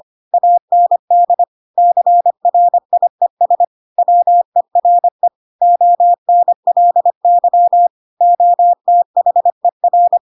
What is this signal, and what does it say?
Morse code, heard as a keyed tone, about 25 wpm, 700 Hz. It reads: AND CRIES WERE ONLY OTHER
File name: Pinocchio / Pinocchio ch7 25 25 ik1hge